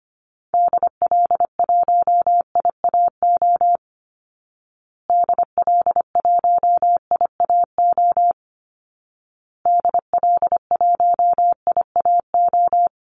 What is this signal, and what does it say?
Morse code, heard as a keyed tone, about 25 wpm, 700 Hz. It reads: DL1IAO DL1IAO DL1IAO
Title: Call Signs Repeated 2x - 25wpm